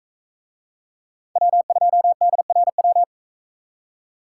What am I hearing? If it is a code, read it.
Morse code, heard as a keyed tone, about 43 wpm, 700 Hz. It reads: W2DRW